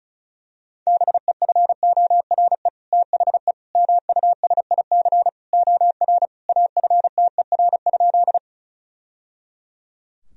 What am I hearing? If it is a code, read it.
Morse code, heard as a keyed tone, about 36 wpm, 700 Hz. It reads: BEFORE THE MUSIC OR AFTER?
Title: Sentences from Top 1000 Words - Review Part 1 - 35wpm